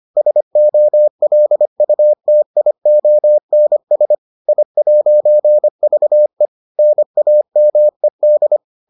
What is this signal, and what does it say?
Morse code, heard as a keyed tone, about 25 wpm, 600 Hz. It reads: SOLUTIONS I'VE NAMED